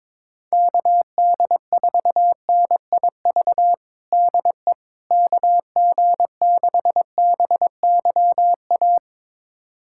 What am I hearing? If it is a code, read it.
Morse code, heard as a keyed tone, about 21 wpm, 700 Hz. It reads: KD4NIV DE KG6BYA
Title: USA Callsign DE Callsign 22wpm